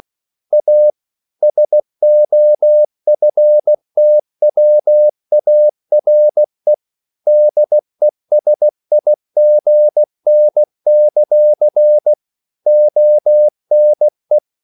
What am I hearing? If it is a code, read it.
Morse code, heard as a keyed tone, about 16 wpm, 600 Hz. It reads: A SOFTWARE DESIGN; ONE
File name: CW / Practice / EnglishQuotations0216